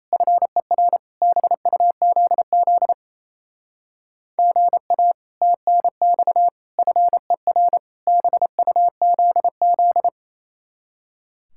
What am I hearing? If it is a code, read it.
Morse code, heard as a keyed tone, about 34 wpm, 700 Hz. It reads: FER BUZZ GA TNX FER BUZZ
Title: Speed-Racing Common QSO Phrases Encore - 28wpm